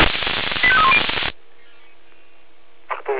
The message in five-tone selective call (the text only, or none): 85310